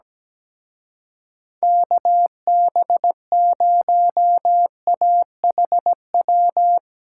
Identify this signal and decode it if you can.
Morse code, heard as a keyed tone, about 17 wpm, 700 Hz. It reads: KB0AHW
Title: Speed Racing - USA Call Signs 17wpm